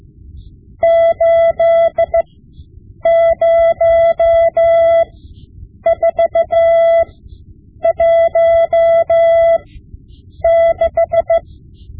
Morse code, heard as a keyed tone, about 15 wpm, spaced as for 8 wpm, 670 Hz. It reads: 80416
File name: Morse Lessons / Morse Lessons / Tape-5B